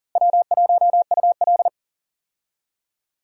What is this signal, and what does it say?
Morse code, heard as a keyed tone, about 39 wpm, 700 Hz. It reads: W1UL
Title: USA Call Signs - 4 Characters - Repeated 3x - 40wpm